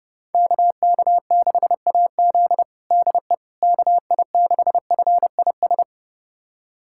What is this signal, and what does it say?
Morse code, heard as a keyed tone, about 30 wpm, 700 Hz. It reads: KK6AZ DE KI6FIS
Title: USA Callsign DE Callsign 30wpm